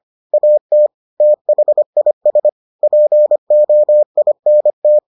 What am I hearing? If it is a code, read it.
Morse code, heard as a keyed tone, about 25 wpm, 600 Hz. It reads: AT THIS POINT